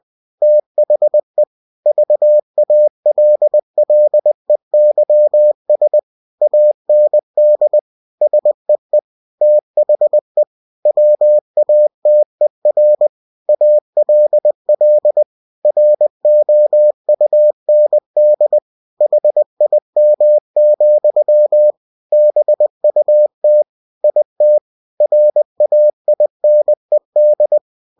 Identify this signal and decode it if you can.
Morse code, heard as a keyed tone, about 20 wpm, 600 Hz. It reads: THE VALLEYS AND SEE THE WATER ALL ROUND HIM, BUT IT RAINED
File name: cw-books / winnie-the-pooh / winnie-20wpm-page-0095